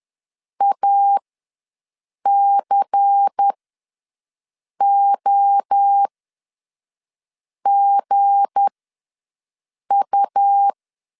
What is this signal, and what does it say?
Morse code, heard as a keyed tone, about 10 wpm, spaced as for 5 wpm, 800 Hz. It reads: ACOGU
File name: LEZIONI / 7 GF9Z / 7 GF ws 10wpm 3918